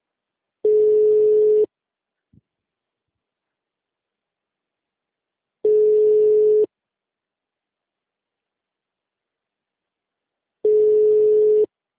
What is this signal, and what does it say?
Ringback tone (Europe, 425 Hz).
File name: out- -2031- - - .10561